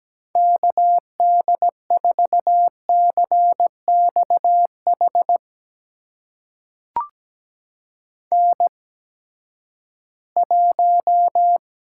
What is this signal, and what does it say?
Morse code, heard as a keyed tone, about 17 wpm, spaced as for 12 wpm, 700 Hz. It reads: KD4CXH N1
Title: USA Call Signs - 8x Character Spacing - 17wpm